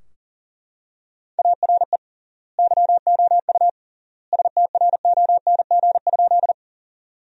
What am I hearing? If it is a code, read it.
Morse code, heard as a keyed tone, about 40 wpm, 700 Hz. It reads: ARE YOU STRONG?
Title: 3x Word-Spacing - Sentences from Top 500 Words - 40wpm